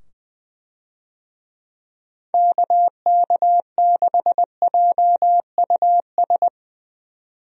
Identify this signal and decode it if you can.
Morse code, heard as a keyed tone, about 20 wpm, 700 Hz. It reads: KK6JUS
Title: USA Call Signs - 6 Characters - 20wpm